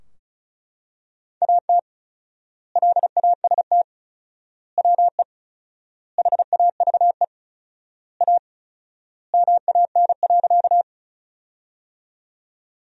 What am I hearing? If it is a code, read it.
Morse code, heard as a keyed tone, about 36 wpm, 700 Hz. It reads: AT LAST WE HAVE A MAN.
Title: 4x Word-Spacing - Sentences from Top 300 Words - 35wpm